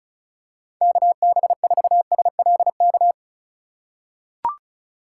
Morse code, heard as a keyed tone, about 36 wpm, 700 Hz. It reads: KB4SLK
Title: USA Call Signs - 6 Characters - 3x Character Spacing - 35wpm